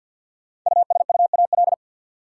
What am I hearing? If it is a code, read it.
Morse code, heard as a keyed tone, about 53 wpm, 700 Hz. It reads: USUAL
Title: Sentences from Top 400 Words - New Word Drill - 50wpm